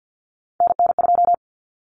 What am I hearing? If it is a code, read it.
Morse code, heard as a keyed tone, about 48 wpm, 700 Hz. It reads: NN2
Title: Common Cut Numbers - 3-Digit 50wpm